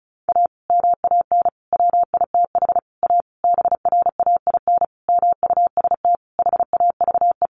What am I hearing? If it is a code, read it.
Morse code, heard as a keyed tone, about 34 wpm, 700 Hz. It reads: A MAN WITH A BRAIN MUST HAVE
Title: Sentences from Top 1100 Words 35wpm